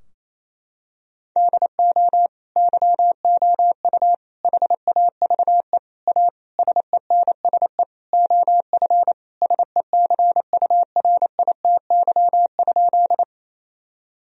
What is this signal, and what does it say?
Morse code, heard as a keyed tone, about 28 wpm, 700 Hz. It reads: DO YOU HAVE A SENSE OF SECURITY?
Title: Sentences from Top 1000 Words - Review Part 4 - 28wpm